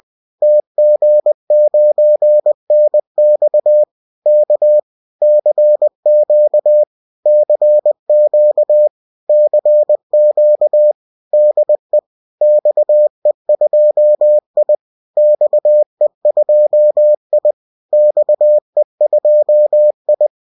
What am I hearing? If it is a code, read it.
Morse code, heard as a keyed tone, about 20 wpm, 600 Hz. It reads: TG9NX K CQ CQ CQ DE XE2I XE2I XE2I